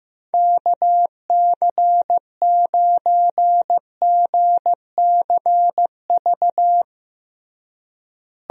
Morse code, heard as a keyed tone, about 16 wpm, 700 Hz. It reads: KC9GCV
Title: USA Call Signs - 6 Characters - Repeated 2x - 15wpm